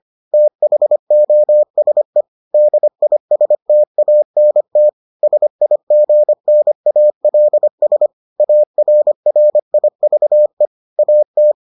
Morse code, heard as a keyed tone, about 25 wpm, 600 Hz. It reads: THOSE DISTANT SIGNALS ARRIVE AT